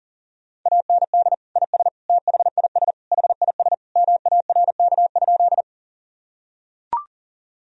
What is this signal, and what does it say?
Morse code, heard as a keyed tone, about 40 wpm, 700 Hz. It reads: AND IS THIS HIS MARK?
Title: Sentences from Top 1000 Words - Review Part 1 - 40wpm